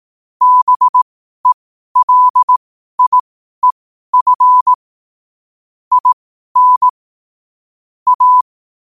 Morse code, heard as a keyed tone, about 18 wpm, spaced as for 12 wpm, 1000 Hz. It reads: BELIEF IN A
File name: qotdfiles / 12wpm